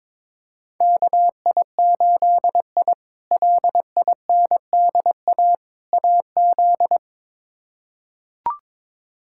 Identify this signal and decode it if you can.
Morse code, heard as a keyed tone, about 22 wpm, 700 Hz. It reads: KI8I LINDA AZ
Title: K1USN SST QSO Practice - 22wpm